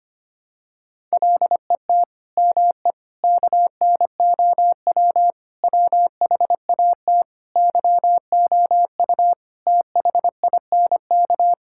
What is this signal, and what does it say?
Morse code, heard as a keyed tone, about 25 wpm, 700 Hz. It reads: LET ME KNOW WHAT YOU THINK
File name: Sets of 9 Words - 25wpm